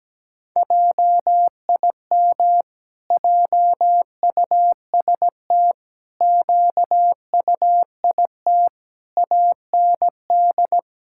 Morse code, heard as a keyed tone, about 17 wpm, 700 Hz. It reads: JIM JUST QUIT AND
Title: Pangram Challenge v2 - 17wpm